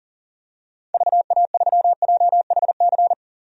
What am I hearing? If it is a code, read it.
Morse code, heard as a keyed tone, about 40 wpm, 700 Hz. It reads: VA3JHC
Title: DXCC-100 Call Signs - 6 Characters - 40wpm